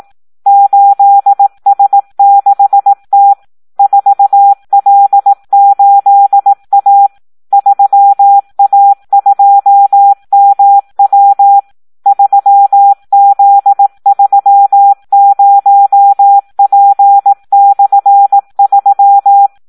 Morse code, heard as a keyed tone, about 18 wpm, 800 Hz. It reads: 8S6T 4L8A 3A2MW 3Z30P/3
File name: Audio practica CW / SD 1 18